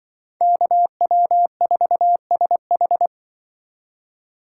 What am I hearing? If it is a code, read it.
Morse code, heard as a keyed tone, about 24 wpm, 700 Hz. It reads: KW4SH
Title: Speed Racing - USA 2x2 Call Signs 20wpm